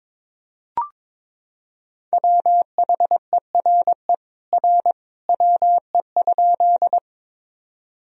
Morse code, heard as a keyed tone, about 21 wpm, 700 Hz. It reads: WHERE R WE?
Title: Sentences from Top 200 Words and Abbreviations 22wpm